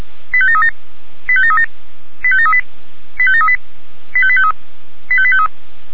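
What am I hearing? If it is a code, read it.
Five-tone selective call (ZVEI), heard as quick runs of five tones, tones of about 80 ms. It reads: 85737 85737 85738 85738 85853 85853
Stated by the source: ZVEI five-tone selective call 85825, received by SDR.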